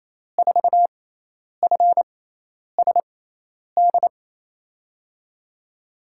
Morse code, heard as a keyed tone, about 27 wpm, spaced as for 10 wpm, 700 Hz. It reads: 4FSD
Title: USA Call Signs - 6x Character Spacing - 28wpm